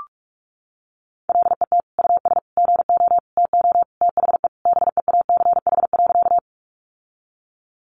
Morse code, heard as a keyed tone, about 44 wpm, 700 Hz. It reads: LET US GO TO THE BEACH.
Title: Sentences from Top 1500 Words 45wpm